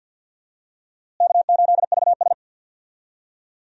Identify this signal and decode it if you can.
Morse code, heard as a keyed tone, about 46 wpm, 700 Hz. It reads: K8VS
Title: USA Call Signs - 4 Characters - Repeated 3x - 50wpm